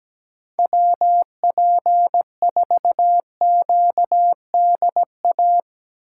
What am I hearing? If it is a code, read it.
Morse code, heard as a keyed tone, about 17 wpm, 700 Hz. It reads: WP4QDA
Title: USA Call Signs - 6 Characters - 4x Character Spacing - 17wpm